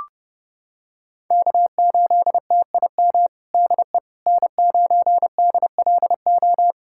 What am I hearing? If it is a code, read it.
Morse code, heard as a keyed tone, about 32 wpm, 700 Hz. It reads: K8TIM DE N9DLO
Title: USA Callsign DE Callsign - 5 Characters - 30wpm